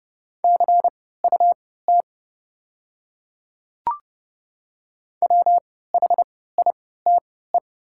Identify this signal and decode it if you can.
Morse code, heard as a keyed tone, about 31 wpm, spaced as for 17 wpm, 700 Hz. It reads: CUT WHITE
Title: 3x Character Spacing - Top 300 Words - 30wpm